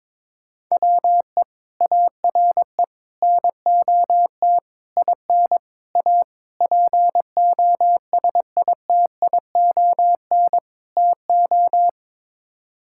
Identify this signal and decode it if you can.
Morse code, heard as a keyed tone, about 21 wpm, 700 Hz. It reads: WE ARE NOT IN A POSITION TO
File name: practice / 22wpm / Sets of 7 Words - 22wpm